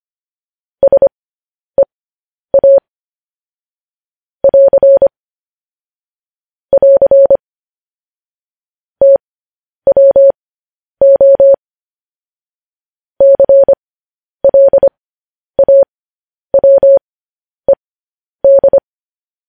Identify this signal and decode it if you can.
Morse code, heard as a keyed tone, about 25 wpm, spaced as for 10 wpm, 550 Hz. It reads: SEA <AR> <AR> TWO CLAWED
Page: news-headlines-in-morse-code-at-10-wpm